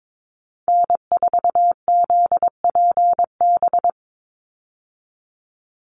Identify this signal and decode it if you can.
Morse code, heard as a keyed tone, about 22 wpm, 700 Hz. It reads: N4ZPB
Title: USA Call Signs 22wpm